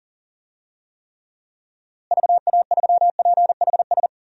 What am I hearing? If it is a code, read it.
Morse code, heard as a keyed tone, about 40 wpm, 700 Hz. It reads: VA3PHS
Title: DXCC-100 Call Signs - 6 Characters - Repeated 3x - 40wpm